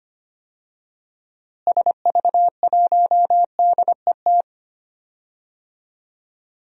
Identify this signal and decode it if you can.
Morse code, heard as a keyed tone, about 25 wpm, 700 Hz. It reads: SV1DET
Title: DXCC-100 Call Signs - 6 Characters - Repeated 3x - 25wpm